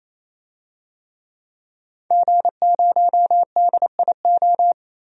Morse code, heard as a keyed tone, about 28 wpm, 700 Hz. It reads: G0DIO
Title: DXCC-100 Call Signs - 5 Characters - 28wpm